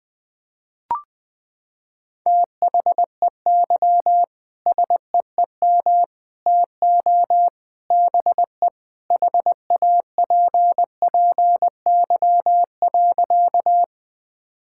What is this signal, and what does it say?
Morse code, heard as a keyed tone, about 21 wpm, 700 Hz. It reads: THEY SEEM TO BE HAPPY.
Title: Sentences with 3-Word Phrases - 20wpm